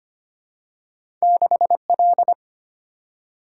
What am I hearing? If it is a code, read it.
Morse code, heard as a keyed tone, about 25 wpm, 700 Hz. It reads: 6L
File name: Two Letter-Number Combinations - 25wpm